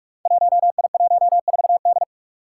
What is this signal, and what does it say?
Morse code, heard as a keyed tone, about 46 wpm, 700 Hz. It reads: 1I14D
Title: Five Challenging Letter-Number Combinations 45wpm